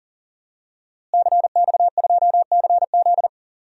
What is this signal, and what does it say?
Morse code, heard as a keyed tone, about 40 wpm, 700 Hz. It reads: CX2CZ
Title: DXCC-100 Call Signs - 5 Characters - Repeated 3x - 40wpm